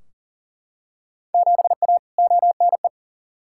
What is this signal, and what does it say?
Morse code, heard as a keyed tone, about 40 wpm, 700 Hz. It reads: 7A ONE
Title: Field Day Exchanges Encore - 40wpm